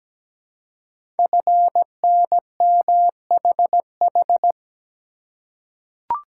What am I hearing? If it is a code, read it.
Morse code, heard as a keyed tone, about 17 wpm, 700 Hz. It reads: FNMHH
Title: Code Groups - 5 Letters - 17wpm